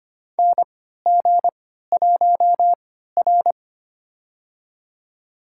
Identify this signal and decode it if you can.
Morse code, heard as a keyed tone, about 25 wpm, spaced as for 14 wpm, 700 Hz. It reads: NG1R
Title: USA Call Signs - 4 Characters - 3x Character Spacing - 25wpm